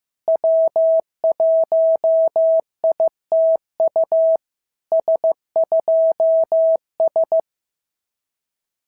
Morse code, heard as a keyed tone, about 15 wpm, 650 Hz. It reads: W1ITU S2S
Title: SOTA QSO Practice 15wpm